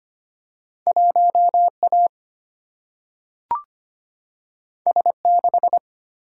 Morse code, heard as a keyed tone, about 25 wpm, 700 Hz. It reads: 1 A S 6
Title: Single-Letter Single-Number 25wpm